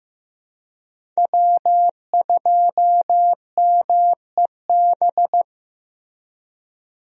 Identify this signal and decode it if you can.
Morse code, heard as a keyed tone, about 15 wpm, 700 Hz. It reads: W2MEB
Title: USA Call Signs - 5 Characters - Repeated 2x - 15wpm